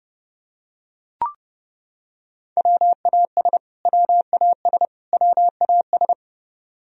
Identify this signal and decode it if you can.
Morse code, heard as a keyed tone, about 31 wpm, 700 Hz. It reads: WAS WAS WAS
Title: Top 100 Words Repeated 30wpm